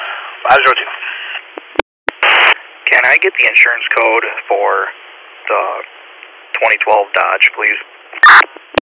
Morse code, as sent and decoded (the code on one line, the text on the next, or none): none